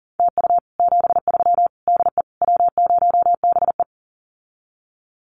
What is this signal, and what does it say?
Morse code, heard as a keyed tone, about 38 wpm, 700 Hz. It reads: TU 73 DE W0BE
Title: Common QSO Phrases Encore 40wpm